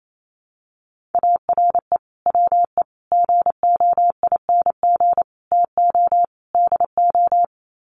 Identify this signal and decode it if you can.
Morse code, heard as a keyed tone, about 28 wpm, 700 Hz. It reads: ARE WE GOING TO DO
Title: Sets of 5 Words 28wpm